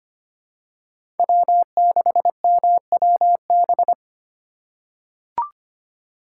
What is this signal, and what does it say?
Morse code, heard as a keyed tone, about 25 wpm, 700 Hz. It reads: W6MWB
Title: USA Call Signs - 5 Characters - 8x Character Spacing - 25wpm